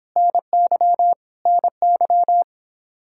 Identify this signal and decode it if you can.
Morse code, heard as a keyed tone, about 26 wpm, 700 Hz. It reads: NY NY